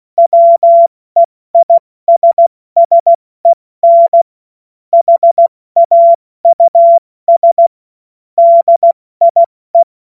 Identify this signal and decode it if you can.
Morse code, heard as a keyed tone, about 16 wpm, 680 Hz. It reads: WEISSEN HAUS DIE
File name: Morsekurs/Witze